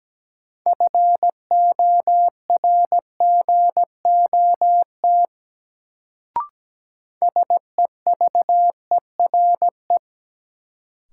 Morse code, heard as a keyed tone, about 17 wpm, 700 Hz. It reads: FORGOT SEVERE
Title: Sentences from Top 1400 Words - New Word Drill - 17wpm